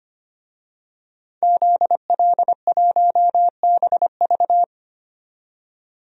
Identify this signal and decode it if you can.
Morse code, heard as a keyed tone, about 25 wpm, 700 Hz. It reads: ZL1BV
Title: DXCC-100 Call Signs - 5 Characters - 25wpm